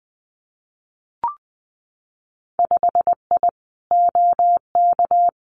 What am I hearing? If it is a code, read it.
Morse code, heard as a keyed tone, about 20 wpm, 700 Hz. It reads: 5I OK
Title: Winter Field Day Exchanges Encore 20wpm